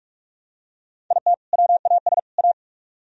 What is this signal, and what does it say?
Morse code, heard as a keyed tone, about 41 wpm, 700 Hz. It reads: IT WAS A